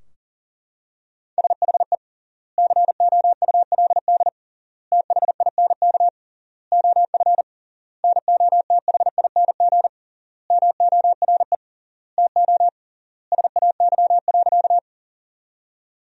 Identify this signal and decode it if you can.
Morse code, heard as a keyed tone, about 40 wpm, 700 Hz. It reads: SHE COULD THINK OF NOTHING MORE TO SAY.
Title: 3x Word-Spacing - Sentences from Top 500 Words - 40wpm